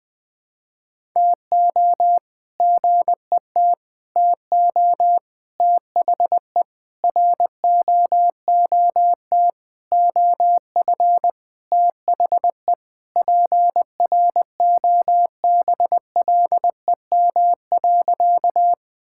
Morse code, heard as a keyed tone, about 20 wpm, 700 Hz. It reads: TO GET TO THE ROOT OF THE PROBLEM.
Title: All-Age Appropriate Jokes - 20wpm